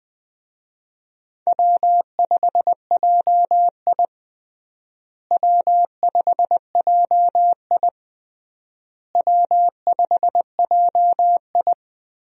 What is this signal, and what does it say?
Morse code, heard as a keyed tone, about 21 wpm, 700 Hz. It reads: W5JI W5JI W5JI
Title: USA Call Signs - 4 Characters - Repeated 3x - 20wpm